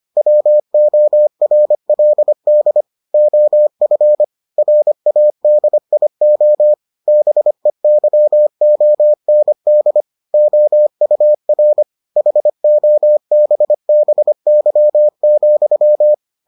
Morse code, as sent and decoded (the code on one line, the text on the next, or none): .-- --- .-. .-.. -.. / --- ..-. / .-. .- -.. .. --- / -... -.-- --- -. -.. / --- ..- .-. / .... --- -... -... -.-- --..--
WORLD OF RADIO BEYOND OUR HOBBY,